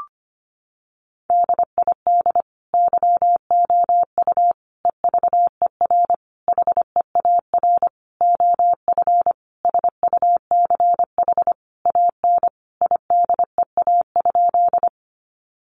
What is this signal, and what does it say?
Morse code, heard as a keyed tone, about 25 wpm, 700 Hz. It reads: DID YOU EVER HEAR OF SUCH AN IDEA?
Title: Sentences from Top 400 Words 25wpm